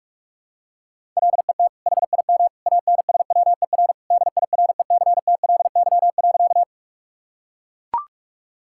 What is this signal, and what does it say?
Morse code, heard as a keyed tone, about 42 wpm, 700 Hz. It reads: LET HIM ANSWER DIRECTLY.
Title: Sentences from Top 1000 Words - Review Part 5 - 45wpm